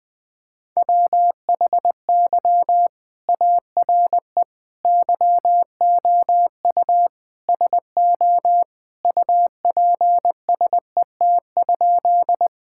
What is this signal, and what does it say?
Morse code, heard as a keyed tone, about 19 wpm, 700 Hz. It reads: WHY ARE YOU SO UPSET?
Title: Sentences from Top 1000 Words - Review Part 5 - 20wpm